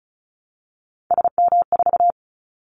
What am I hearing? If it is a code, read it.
Morse code, heard as a keyed tone, about 34 wpm, 700 Hz. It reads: SM4
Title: Callsign Prefixes for Top 25 DXCC - Repeated 3x - 35wpm